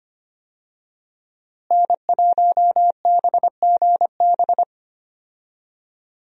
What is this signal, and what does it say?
Morse code, heard as a keyed tone, about 25 wpm, 700 Hz. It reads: N1BGB
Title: USA Call Signs - 5 Characters - Repeated 2x - 25wpm